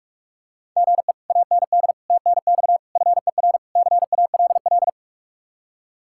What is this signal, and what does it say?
Morse code, heard as a keyed tone, about 49 wpm, 700 Hz. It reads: GE AND TNX FER CALL